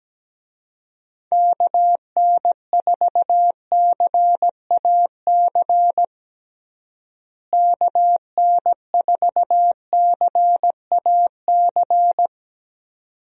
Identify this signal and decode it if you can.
Morse code, heard as a keyed tone, about 17 wpm, 700 Hz. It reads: KN4CAC KN4CAC
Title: USA Call Signs - 6 Characters - Repeated 2x - 17wpm